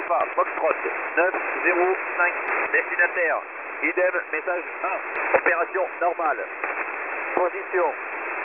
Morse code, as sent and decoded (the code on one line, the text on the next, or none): none